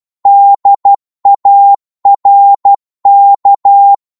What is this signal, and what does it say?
Morse code, heard as a keyed tone, about 12 wpm, 800 Hz. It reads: DARK
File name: cwmp3 / psg-1-12wpm